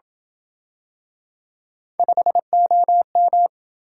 Morse code, heard as a keyed tone, about 27 wpm, 700 Hz. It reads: 5OM